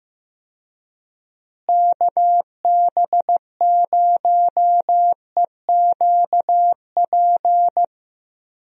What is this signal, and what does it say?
Morse code, heard as a keyed tone, about 15 wpm, 700 Hz. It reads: KB0EQP
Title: USA Call Signs - 6 Characters - Repeated 2x - 15wpm